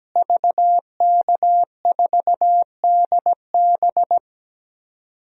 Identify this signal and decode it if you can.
Morse code, heard as a keyed tone, about 17 wpm, 700 Hz. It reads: VK4DB
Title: DXCC-100 Call Signs - 5 Characters - Repeated 3x - 17wpm